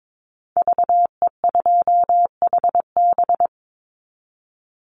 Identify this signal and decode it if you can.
Morse code, heard as a keyed tone, about 22 wpm, 700 Hz. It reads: VE2HB
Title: Call Signs - 22wpm